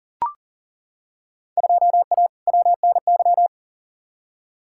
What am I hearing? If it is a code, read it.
Morse code, heard as a keyed tone, about 40 wpm, 700 Hz. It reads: 2A WNY
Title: Field Day Exchanges - 40wpm